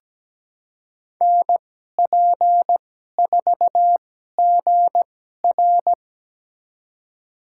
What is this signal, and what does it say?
Morse code, heard as a keyed tone, about 17 wpm, spaced as for 12 wpm, 700 Hz. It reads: NP4GR
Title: USA Call Signs - 2x Character Spacing - 17wpm